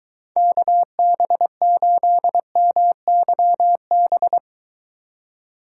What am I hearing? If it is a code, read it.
Morse code, heard as a keyed tone, about 22 wpm, 700 Hz. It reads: KB8MYB